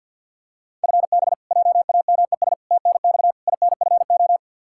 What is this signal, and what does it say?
Morse code, heard as a keyed tone, about 51 wpm, 700 Hz. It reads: FB JAMES TNX INFO
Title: Practice QSO - Exchange 3 - More Info - 50wpm